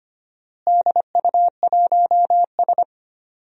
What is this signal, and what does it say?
Morse code, heard as a keyed tone, about 25 wpm, 700 Hz. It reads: DU1S
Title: DXCC-100 Call Signs - 4 Characters - Repeated 2x - 25wpm